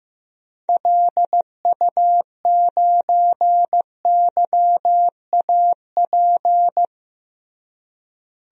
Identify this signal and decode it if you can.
Morse code, heard as a keyed tone, about 16 wpm, 700 Hz. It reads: LU9YAP
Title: DXCC-100 Call Signs - 6 Characters - Repeated 3x - 15wpm